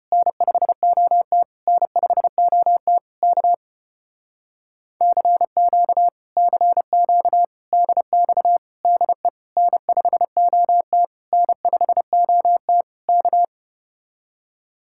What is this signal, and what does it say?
Morse code, heard as a keyed tone, about 33 wpm, 700 Hz. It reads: N5OT N5OT K CQ CQ DX DE N5OT N5OT K